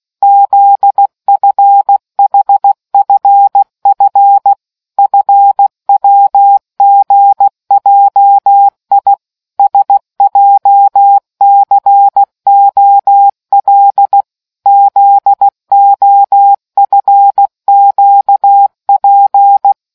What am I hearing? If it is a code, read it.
Morse code, heard as a keyed tone, about 16 wpm, 800 Hz. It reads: ZFHFF FWGJI SJCOL ZOFQP